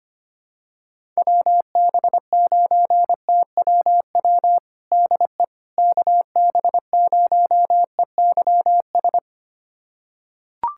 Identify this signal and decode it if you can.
Morse code, heard as a keyed tone, about 25 wpm, 700 Hz. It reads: WB9TWW DE KB0EYS
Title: USA Callsign DE Callsign - 6 Characters - 25wpm